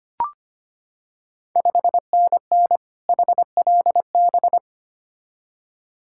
Morse code, heard as a keyed tone, about 25 wpm, 690 Hz. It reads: 5NN HLB
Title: Ontario QSO Party Exchange 25wpm